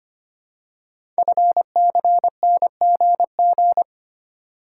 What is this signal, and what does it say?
Morse code, heard as a keyed tone, about 25 wpm, 690 Hz. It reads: FCNGG
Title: Code Groups - 5 Letters - 25wpm